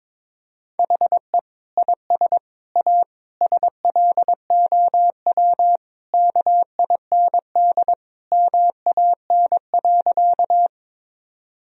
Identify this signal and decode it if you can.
Morse code, heard as a keyed tone, about 22 wpm, 700 Hz. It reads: HE IS A SLOW KIND MAN.
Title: Sentences from Top 1000 Words - Review Part 2 - 22wpm